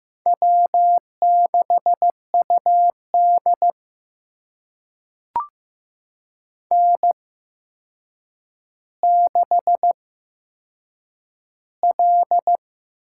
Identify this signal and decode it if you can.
Morse code, heard as a keyed tone, about 15 wpm, spaced as for 7 wpm, 700 Hz. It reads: W6UD N6L
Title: USA Call Signs - 4 Characters - 8x Character Spacing - 15wpm